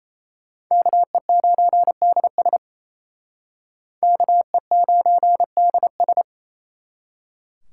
Morse code, heard as a keyed tone, about 35 wpm, 700 Hz. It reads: KE9DS KE9DS
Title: Speed Racing - USA Call Signs 28wpm